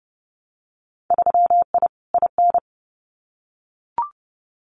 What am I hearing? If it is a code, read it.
Morse code, heard as a keyed tone, about 30 wpm, 700 Hz. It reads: 3I IN
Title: Winter Field Day Exchanges 30wpm